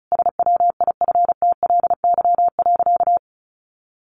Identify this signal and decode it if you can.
Morse code, heard as a keyed tone, about 34 wpm, 700 Hz. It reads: SWIFTLY.
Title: A Sound Of Thunder 35wpm